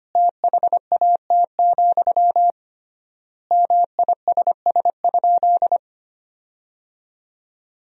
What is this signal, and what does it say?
Morse code, heard as a keyed tone, about 25 wpm, 700 Hz. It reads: THAT, MISS?
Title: 3x Word-Spacing - Sentences from Top 500 Words - 25wpm